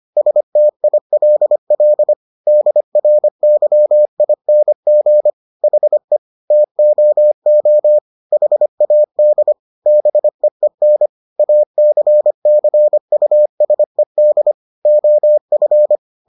Morse code, heard as a keyed tone, about 25 wpm, 600 Hz. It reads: STILL DRYING HE TOO HAD BEEN ACCUSED OF